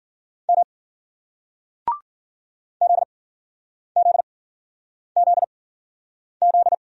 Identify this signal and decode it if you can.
Morse code, heard as a keyed tone, about 56 wpm, spaced as for 9 wpm, 700 Hz. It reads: M ZZZZ